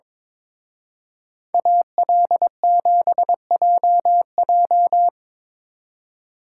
Morse code, heard as a keyed tone, about 25 wpm, 700 Hz. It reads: AL7JJ
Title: Speed Racing - USA 2x2 Call Signs 22wpm